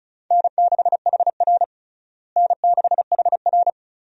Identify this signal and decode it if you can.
Morse code, heard as a keyed tone, about 34 wpm, 700 Hz. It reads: N6HR N6HR
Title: USA Call Signs - 4 Characters - Repeated 2x - 35wpm